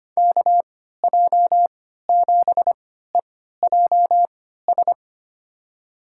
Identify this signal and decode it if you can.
Morse code, heard as a keyed tone, about 25 wpm, spaced as for 14 wpm, 700 Hz. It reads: KJ7EJS
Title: USA Call Signs - 6 Characters - 3x Character Spacing - 25wpm